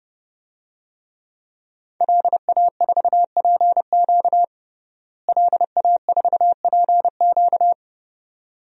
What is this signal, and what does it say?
Morse code, heard as a keyed tone, about 30 wpm, 700 Hz. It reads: LA4PQ LA4PQ
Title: DXCC-100 Call Signs - 5 Characters - Repeated 2x - 30wpm